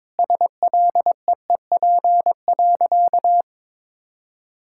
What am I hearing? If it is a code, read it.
Morse code, heard as a keyed tone, about 22 wpm, 700 Hz. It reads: SLEEP.
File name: practice / 22wpm / Sentences with 3-Word Phrases - 22wpm